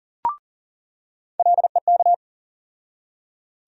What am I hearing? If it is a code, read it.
Morse code, heard as a keyed tone, about 38 wpm, 700 Hz. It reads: LEK
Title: USA Call Sign Postfixes - 3 Letters 40wpm